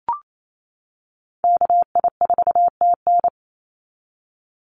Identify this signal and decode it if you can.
Morse code, heard as a keyed tone, about 28 wpm, 700 Hz. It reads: KI4TN
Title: NA SOTA Chaser Callsigns 28wpm